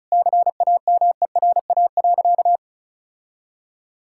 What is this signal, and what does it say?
Morse code, heard as a keyed tone, about 36 wpm, 700 Hz. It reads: CAMERA.
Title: Sentences from Top 1000 Words - Review Part 5 - 35wpm